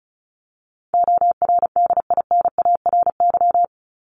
Morse code, heard as a keyed tone, about 34 wpm, 700 Hz. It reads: ORDINARY
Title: Top 1751-2000 Words 35wpm